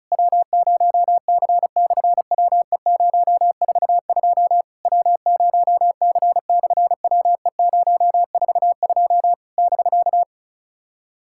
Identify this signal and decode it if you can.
Morse code, heard as a keyed tone, about 33 wpm, 700 Hz. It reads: W0C/WE042 W0C/WE042 <BK>
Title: SOTA QSO Practice 35wpm